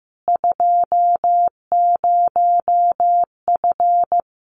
Morse code, heard as a keyed tone, about 15 wpm, 700 Hz. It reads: 20F
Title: Common QSO Phrases Encore 15wpm